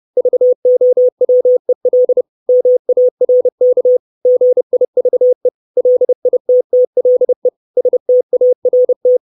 Morse code, heard as a keyed tone, about 30 wpm, 500 Hz. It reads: VOWEL MARK GIVE LITTLE START
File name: 632 Words 30WPM